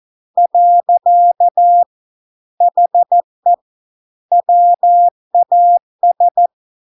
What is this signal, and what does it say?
Morse code, heard as a keyed tone, about 14 wpm, 700 Hz. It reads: . HE WAS